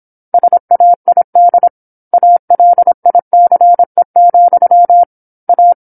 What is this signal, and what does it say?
Morse code, heard as a keyed tone, about 26 wpm, 700 Hz. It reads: SAID ALICE, A
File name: 08-Chapter-08 26 WPM